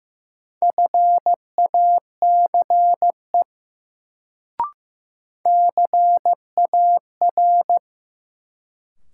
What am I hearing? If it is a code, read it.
Morse code, heard as a keyed tone, about 15 wpm, 700 Hz. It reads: FACE CAR
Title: Sentences from Top 300 Words - New Word Drill - 15wpm